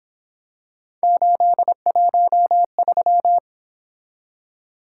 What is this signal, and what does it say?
Morse code, heard as a keyed tone, about 26 wpm, 700 Hz. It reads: 813
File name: Speed-Racing - Three Letter-Number Combinations - 22wpm